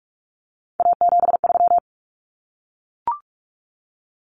Morse code, heard as a keyed tone, about 44 wpm, 700 Hz. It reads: A73
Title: Common Cut Numbers - 3-Digit 45wpm